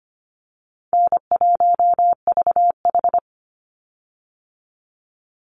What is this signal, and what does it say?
Morse code, heard as a keyed tone, about 25 wpm, 700 Hz. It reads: N1VH
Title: Easy Call Signs Repeated 2x - 25wpm